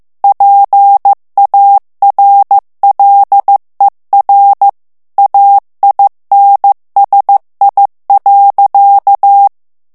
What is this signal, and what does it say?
Morse code, heard as a keyed tone, about 15 wpm, 800 Hz. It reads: PARLER AINSI.
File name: morse mp3 / FRANCAIS TEXTE CLAIR 2 VITESSE 14 MOTS MINUTES